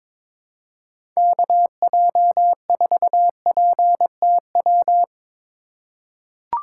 Morse code, heard as a keyed tone, about 22 wpm, 700 Hz. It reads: KJ4PTW
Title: USA Call Signs - 6 Characters - 2x Character Spacing - 22wpm